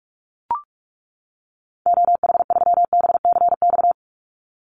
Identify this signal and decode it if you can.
Morse code, heard as a keyed tone, about 44 wpm, 700 Hz. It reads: OH3BCX
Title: Call Signs - 45wpm